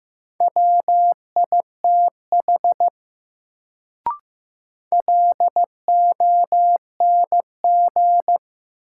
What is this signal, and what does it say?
Morse code, heard as a keyed tone, about 15 wpm, 700 Hz. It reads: WITH LONG
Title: Sentences from Top 100 Words - New Word Drill - 15wpm